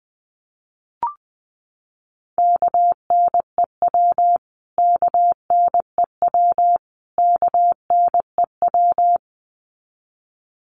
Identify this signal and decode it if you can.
Morse code, heard as a keyed tone, about 20 wpm, 700 Hz. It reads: KNEW KNEW KNEW
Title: Top 500 Words Repeated 20wpm